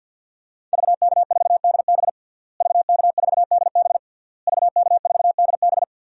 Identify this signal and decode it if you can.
Morse code, heard as a keyed tone, about 52 wpm, 700 Hz. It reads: VK4DB VK4DB VK4DB
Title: DXCC-100 Call Signs - 5 Characters - Repeated 3x - 50wpm